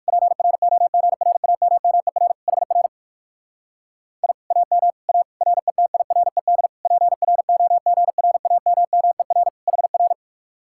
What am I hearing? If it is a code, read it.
Morse code, heard as a keyed tone, about 65 wpm, spaced as for 56 wpm, 700 Hz. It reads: PROGRAMMER HR I AM A RETIRED PROGRAMMER HR
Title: Speed-Racing Common QSO Phrases Encore - 45wpm